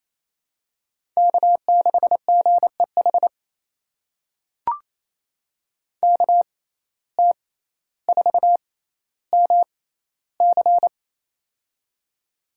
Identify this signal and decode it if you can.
Morse code, heard as a keyed tone, about 28 wpm, spaced as for 14 wpm, 700 Hz. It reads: K6GEH KT4MC
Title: USA Call Signs - 5 Characters - 6x Character Spacing - 28wpm